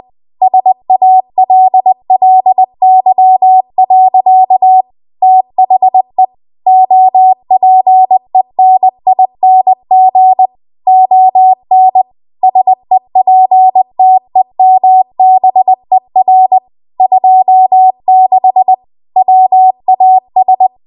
Morse code, heard as a keyed tone, about 20 wpm, 750 Hz. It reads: SALLY. THE OPENING ON SEPTEMBER 26 WAS
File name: Archive / 20 WPM / 20WPM